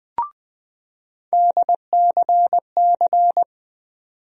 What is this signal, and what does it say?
Morse code, heard as a keyed tone, about 21 wpm, 700 Hz. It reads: DCC
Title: USA Call Sign Postfixes - 3 Letters 20wpm